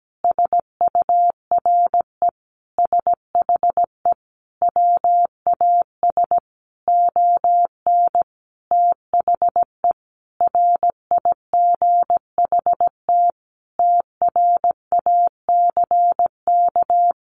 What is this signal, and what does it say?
Morse code, heard as a keyed tone, about 17 wpm, 700 Hz. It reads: SURE SHE WAS ON THE RIGHT TRACK
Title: Sentences from Top 1100 Words 17wpm